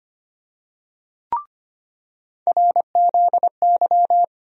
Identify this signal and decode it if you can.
Morse code, heard as a keyed tone, about 25 wpm, 700 Hz. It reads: RZY
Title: USA Call Sign Postfixes - 3 Letters 25wpm